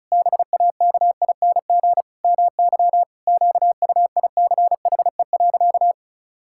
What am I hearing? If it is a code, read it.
Morse code, heard as a keyed tone, about 36 wpm, 700 Hz. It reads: BAKING MY QUICHE.
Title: Pangram Challenge v2 - 35wpm